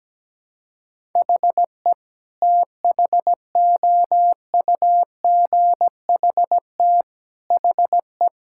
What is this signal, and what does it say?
Morse code, heard as a keyed tone, about 17 wpm, 700 Hz. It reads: HE THOUGHT HE
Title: Sentences from Top 1000 Words - Review Part 6 - 17wpm